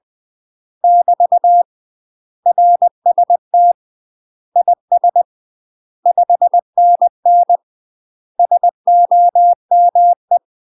Morse code, heard as a keyed tone, about 20 wpm, 700 Hz. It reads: <BT> RST IS 5NN SOME
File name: UK MASTER QSOs TRAINING/20WPM